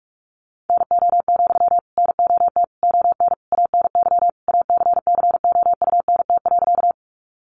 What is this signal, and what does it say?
Morse code, heard as a keyed tone, about 44 wpm, 700 Hz. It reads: NO, NOT ON ANY ACCOUNT.
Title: Sentences from Top 900 Words 45wpm